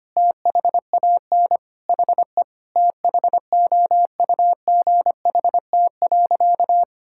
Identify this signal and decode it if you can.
Morse code, heard as a keyed tone, about 25 wpm, 700 Hz. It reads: THAN HE THOUGHT.
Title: Sentences from Top 1600 Words 25wpm